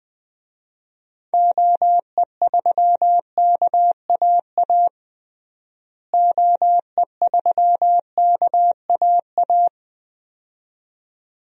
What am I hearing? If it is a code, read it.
Morse code, heard as a keyed tone, about 20 wpm, 700 Hz. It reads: OE3KAA OE3KAA
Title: DXCC-100 Call Signs - 6 Characters - Repeated 2x - 20wpm